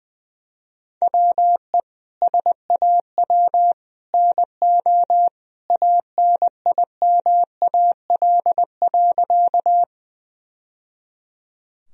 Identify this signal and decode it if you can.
Morse code, heard as a keyed tone, about 21 wpm, 700 Hz. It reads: WE SAW NO ANIMAL.
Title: Sentences from Top 1000 Words - Review Part 1 - 20wpm